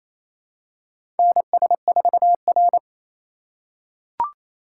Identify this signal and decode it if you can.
Morse code, heard as a keyed tone, about 28 wpm, 700 Hz. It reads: NS4R
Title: USA Call Signs - 4 Characters - 4x Character Spacing - 28wpm